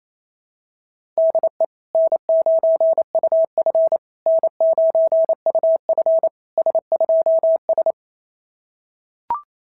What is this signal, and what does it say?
Morse code, heard as a keyed tone, about 25 wpm, 650 Hz. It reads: DE N9UF N9UF S2S
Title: SOTA QSO Practice 28wpm